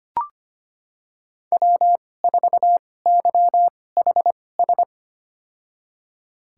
Morse code, heard as a keyed tone, about 25 wpm, spaced as for 18 wpm, 700 Hz. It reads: W4YHS
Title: USA Call Signs - 2x Character Spacing - 25wpm